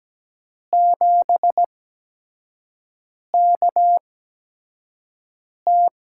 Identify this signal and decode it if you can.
Morse code, heard as a keyed tone, about 17 wpm, spaced as for 5 wpm, 700 Hz. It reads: 7KT
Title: USA Call Signs - 5 Characters - 8x Character Spacing - 17wpm